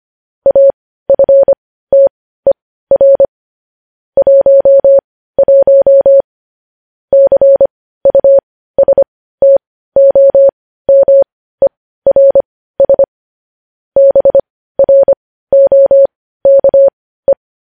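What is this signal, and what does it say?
Morse code, heard as a keyed tone, about 25 wpm, spaced as for 15 wpm, 550 Hz. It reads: AFTER 11 CUSTOMERS BROKE
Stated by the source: Morse code news headlines sent at 15, preceded by vvv vvv.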